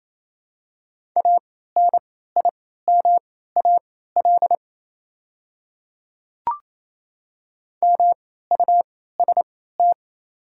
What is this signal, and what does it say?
Morse code, heard as a keyed tone, about 28 wpm, spaced as for 16 wpm, 700 Hz. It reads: ANIMAL MUST
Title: 3x Character Spacing - Top 200 Words - 28wpm